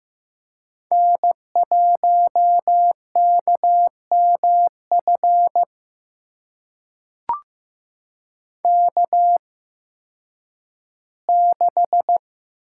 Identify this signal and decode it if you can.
Morse code, heard as a keyed tone, about 15 wpm, spaced as for 10 wpm, 700 Hz. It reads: N1KMF K6
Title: USA Call Signs - 8x Character Spacing - 15wpm